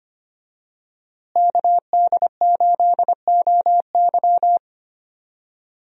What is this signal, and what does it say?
Morse code, heard as a keyed tone, about 25 wpm, 700 Hz. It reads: KD8OY
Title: USA Call Signs - 5 Characters - Repeated 3x - 25wpm